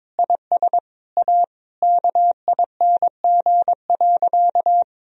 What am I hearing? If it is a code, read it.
Morse code, heard as a keyed tone, about 22 wpm, 700 Hz. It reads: IS A KING.
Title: Sentences from Top 1000 Words - Review Part 3 - 22wpm